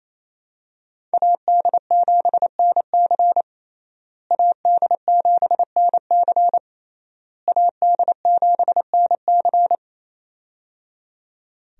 Morse code, heard as a keyed tone, about 28 wpm, 700 Hz. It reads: AD7NC AD7NC AD7NC
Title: USA Call Signs - 5 Characters - Repeated 3x - 28wpm